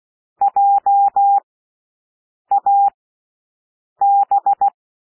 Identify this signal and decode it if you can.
Morse code, heard as a keyed tone, about 16 wpm, spaced as for 6 wpm, 800 Hz. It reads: JAB
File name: cwmp3 / W1 SOUND WORDS 16 wpm